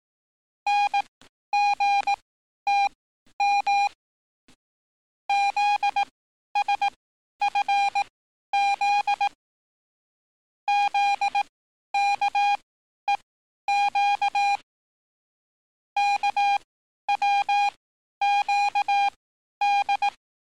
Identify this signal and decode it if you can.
Morse code, heard as a keyed tone, about 18 wpm, spaced as for 11 wpm, 810 Hz. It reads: NGTM ZSFZ ZKEQ KWQD
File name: training / NAVPERS 11049to11060 / NAVPERS 11051 RA RB